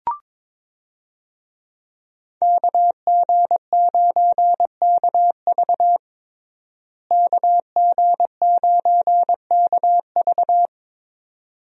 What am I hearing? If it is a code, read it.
Morse code, heard as a keyed tone, about 22 wpm, 700 Hz. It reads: KG9KV KG9KV
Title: USA Call Signs - 5 Characters - Repeated 2x - 22wpm